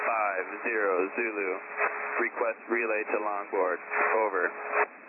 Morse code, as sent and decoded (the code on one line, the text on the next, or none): none